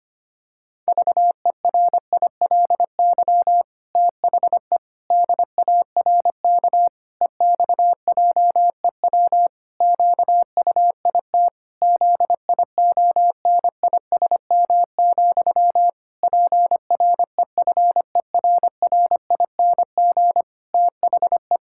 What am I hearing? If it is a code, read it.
Morse code, heard as a keyed tone, about 25 wpm, 700 Hz. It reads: VERILY THE DARK EXJEW QUIT ZIONISM, PREFERRING THE
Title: Pangram Challenge v2 - 25wpm